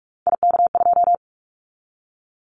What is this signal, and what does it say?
Morse code, heard as a keyed tone, about 46 wpm, 700 Hz. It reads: IK2
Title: Callsign Prefixes for Top 25 DXCC - 45wpm